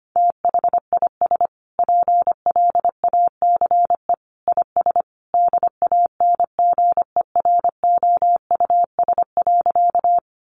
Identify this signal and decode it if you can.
Morse code, heard as a keyed tone, about 25 wpm, 700 Hz. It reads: THIS PLACE IS DANGEROUS.
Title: Sentences from Top 1200 Words 25wpm